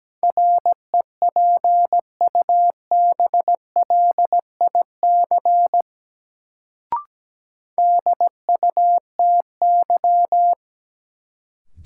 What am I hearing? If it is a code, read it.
Morse code, heard as a keyed tone, about 17 wpm, 700 Hz. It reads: REPUBLIC DUTY
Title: Sentences from Top 1500 Words - New Word Drill - 17wpm